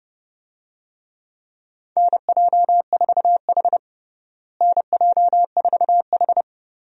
Morse code, heard as a keyed tone, about 30 wpm, 700 Hz. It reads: NJ4H NJ4H
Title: USA Call Signs - 4 Characters - Repeated 2x - 30wpm